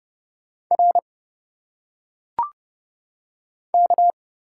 Morse code, heard as a keyed tone, about 30 wpm, 700 Hz. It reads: R K
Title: Single Letters - Phonetic Pronunciation 30wpm